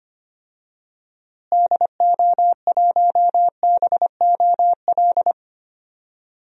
Morse code, heard as a keyed tone, about 25 wpm, 700 Hz. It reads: DO1BOL
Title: DXCC-100 Call Signs - 6 Characters - Repeated 2x - 25wpm